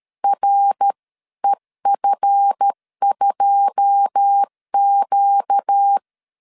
Morse code, heard as a keyed tone, about 13 wpm, 800 Hz. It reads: REF2Q
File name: LEZIONI / 9 286 12wpm 2305